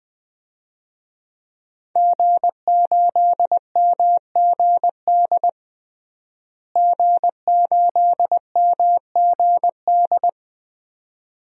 Morse code, heard as a keyed tone, about 19 wpm, 700 Hz. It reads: G8MGD G8MGD
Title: DXCC-100 Call Signs - 5 Characters - Repeated 2x - 20wpm